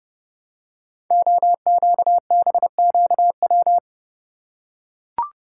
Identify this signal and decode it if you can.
Morse code, heard as a keyed tone, about 30 wpm, 700 Hz. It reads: OQBQW
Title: Code Groups - 5 Letters - 30wpm